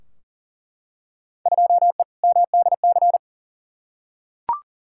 Morse code, heard as a keyed tone, about 40 wpm, 700 Hz. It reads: 2E MDC
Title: Field Day Exchanges - 40wpm